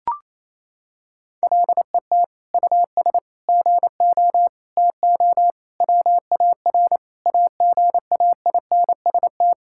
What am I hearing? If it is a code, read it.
Morse code, heard as a keyed tone, about 28 wpm, 700 Hz. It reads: LET US GO TO WAR AGAINST
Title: Sentences from Top 1000 Words - Review Part 5 - 28wpm